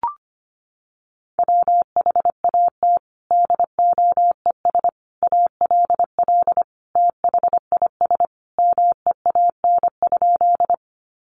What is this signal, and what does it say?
Morse code, heard as a keyed tone, about 25 wpm, 700 Hz. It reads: WHAT DOES ALL THIS MEAN?
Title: Sentences from Top 200 Words 25wpm